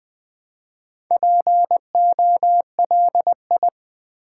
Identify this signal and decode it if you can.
Morse code, heard as a keyed tone, about 20 wpm, 700 Hz. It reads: POLI
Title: Build A Word - 2 Letters At A Time - 8-Letter Words - 20wpm